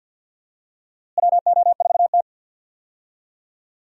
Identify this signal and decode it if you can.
Morse code, heard as a keyed tone, about 49 wpm, 700 Hz. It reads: WO4T
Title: USA Call Signs - 4 Characters - Repeated 3x - 50wpm